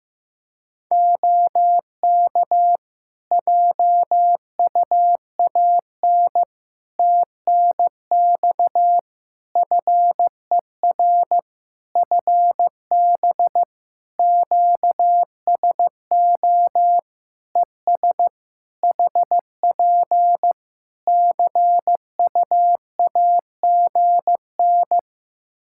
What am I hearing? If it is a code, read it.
Morse code, heard as a keyed tone, about 15 wpm, 700 Hz. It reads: OK JUAN TNX FER FB QSO ES HP CU AGN
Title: Practice QSO - Exchange 4 - Ending - 15wpm